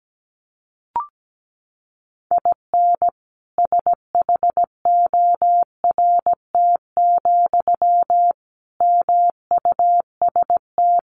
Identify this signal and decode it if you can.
Morse code, heard as a keyed tone, about 17 wpm, 700 Hz. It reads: IN SHORT, MUST
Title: Sentences from Top 1000 Words 17wpm